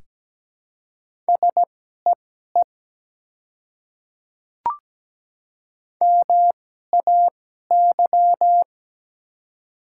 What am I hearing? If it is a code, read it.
Morse code, heard as a keyed tone, about 17 wpm, spaced as for 12 wpm, 700 Hz. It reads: SEE MAY
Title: 2x Character Spacing - Top 100 Words - 17wpm